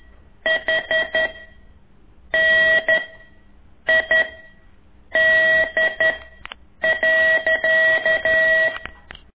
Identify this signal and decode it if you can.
Morse code, heard as a keyed tone, about 11 wpm, 1950 Hz. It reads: H N I D .
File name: ik1weg2 / CORSI CW